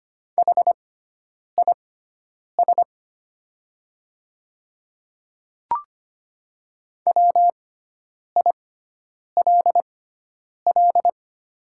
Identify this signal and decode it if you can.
Morse code, heard as a keyed tone, about 25 wpm, spaced as for 9 wpm, 700 Hz. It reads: HIS WILL
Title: 6x Character Spacing - Top 500 Words - 25wpm